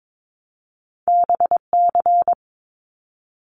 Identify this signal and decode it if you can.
Morse code, heard as a keyed tone, about 22 wpm, 700 Hz. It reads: BC
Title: Canadian Provinces and Territories 22wpm